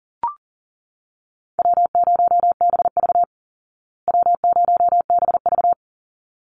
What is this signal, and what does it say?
Morse code, heard as a keyed tone, about 40 wpm, 700 Hz. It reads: W0BV W0BV
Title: Easy Call Signs Repeated 2x - 40wpm